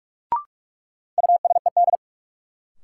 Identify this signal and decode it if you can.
Morse code, heard as a keyed tone, about 46 wpm, 700 Hz. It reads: USED
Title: Sentences from Top 600 Words - New Word Drill - 45wpm